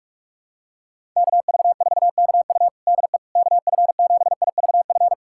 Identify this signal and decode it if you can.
Morse code, heard as a keyed tone, about 47 wpm, 700 Hz. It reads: KV4KU DE KF7IVF
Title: USA Callsign DE Callsign 45wpm